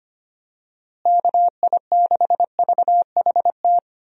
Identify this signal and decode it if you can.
Morse code, heard as a keyed tone, about 25 wpm, 700 Hz. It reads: KI6VHT
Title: USA Call Signs - 6 Characters - Repeated 3x - 25wpm